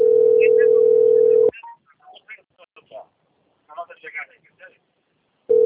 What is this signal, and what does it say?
Ringback tone (North America), 440 + 480 Hz, with speech.
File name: out- - - - - .128